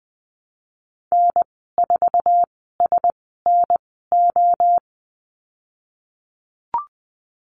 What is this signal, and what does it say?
Morse code, heard as a keyed tone, about 20 wpm, spaced as for 14 wpm, 700 Hz. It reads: N4SNO
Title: 055 Lesson 10g - Letter r - Call Signs 20wpm